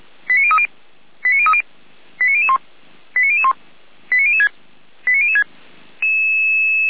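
Five-tone selective call (ZVEI), heard as quick runs of five tones, tones of about 70 ms. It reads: 89030 89030 89001 89001 89006 89006
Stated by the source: ZVEI five-tone selective call received on an SDR, from feuerwehr-orlamuende fire brigade server.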